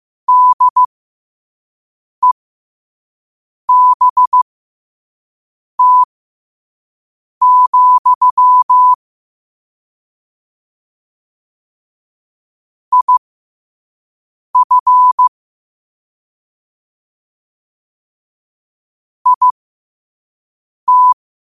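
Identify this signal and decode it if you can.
Morse code, heard as a keyed tone, about 15 wpm, spaced as for 5 wpm, 1000 Hz. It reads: DEBT, IF IT
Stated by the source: Morse code practice at 5 wpm.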